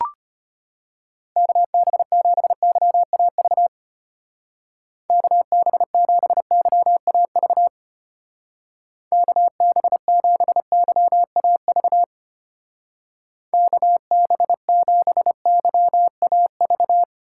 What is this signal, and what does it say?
Morse code, heard as a keyed tone, about 31 wpm, 700 Hz. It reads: KB7YAV KB7YAV KB7YAV KB7YAV
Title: Speed Racing - USA 2x3 Call Signs 25wpm